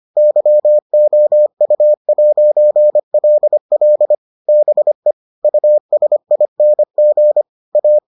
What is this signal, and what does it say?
Morse code, heard as a keyed tone, about 25 wpm, 600 Hz. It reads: YOU'LL BE USING A